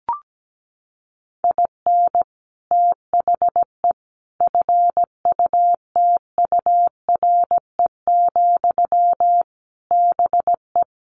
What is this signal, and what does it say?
Morse code, heard as a keyed tone, about 17 wpm, 700 Hz. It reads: IN THE FUTURE, BE
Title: Sentences from Top 1300 Words 17wpm